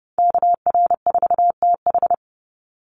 Morse code, heard as a keyed tone, about 30 wpm, 700 Hz. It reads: KR4TH
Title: Call Signs - 30wpm